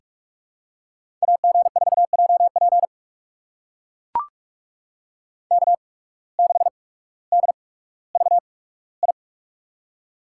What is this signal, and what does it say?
Morse code, heard as a keyed tone, about 41 wpm, spaced as for 19 wpm, 700 Hz. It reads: AG4JP K6DVI
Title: USA Call Signs - 5 Characters - 8x Character Spacing - 45wpm